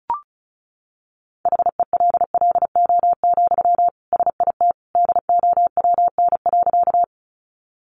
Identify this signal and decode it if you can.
Morse code, heard as a keyed tone, about 34 wpm, 700 Hz. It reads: HELLO, SIT DOWN.
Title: Sentences from Top 1100 Words 35wpm